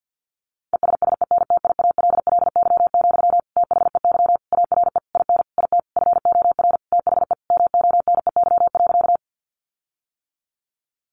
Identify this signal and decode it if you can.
Morse code, heard as a keyed tone, about 50 wpm, 700 Hz. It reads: ESSENTIALLY, THEY ARE IN IT FOR THE MONEY.
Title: Sentences from Top 1300 Words 50wpm